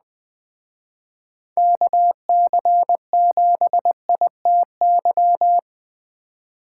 Morse code, heard as a keyed tone, about 20 wpm, 700 Hz. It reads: KC7ITY
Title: Speed Racing - USA 2x3 Call Signs 17wpm